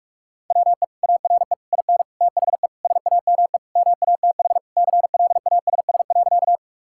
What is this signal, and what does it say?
Morse code, heard as a keyed tone, about 46 wpm, 700 Hz. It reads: WE ARE IN THE SAME MATH CLASS.
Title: Sentences from Top 1000 Words - Review Part 6 - 45wpm